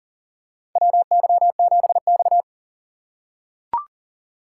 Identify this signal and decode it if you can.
Morse code, heard as a keyed tone, about 37 wpm, 700 Hz. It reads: WY7X
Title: USA Call Signs - 4 Characters - 3x Character Spacing - 40wpm